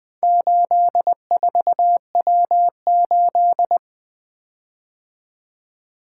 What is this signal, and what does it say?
Morse code, heard as a keyed tone, about 23 wpm, 700 Hz. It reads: 84W8
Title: Speed-Racing - Four Letter-Number Combinations - 20wpm